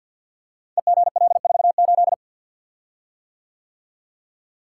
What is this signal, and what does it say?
Morse code, heard as a keyed tone, about 51 wpm, 700 Hz. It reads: EGL48
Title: Code Groups - 5 Letter:Numbers - 50wpm